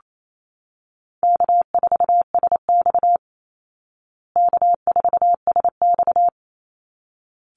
Morse code, heard as a keyed tone, about 28 wpm, 700 Hz. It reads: K4SX K4SX
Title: Easy Call Signs Repeated 2x - 28wpm